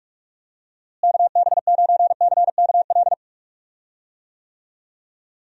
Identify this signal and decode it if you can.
Morse code, heard as a keyed tone, about 41 wpm, 700 Hz. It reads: KB9CKL
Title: USA Call Signs - 6 Characters - Repeated 2x - 45wpm